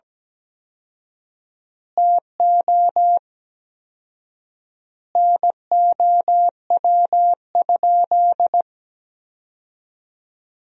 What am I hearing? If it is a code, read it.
Morse code, heard as a keyed tone, about 17 wpm, 700 Hz. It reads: TO NOW?
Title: 4x Word-Spacing - Sentences from Top 200 Words - 17wpm